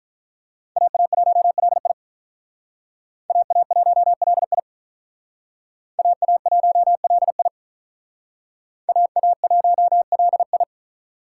Morse code, heard as a keyed tone, about 43 wpm, 700 Hz. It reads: AA1LI AA1LI AA1LI AA1LI
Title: Speed Racing - USA 2x2 Call Signs 35wpm